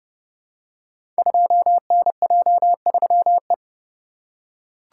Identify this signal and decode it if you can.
Morse code, heard as a keyed tone, about 30 wpm, 700 Hz. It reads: 2NJ3E
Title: Code Groups - 5 Letter:Numbers - 30wpm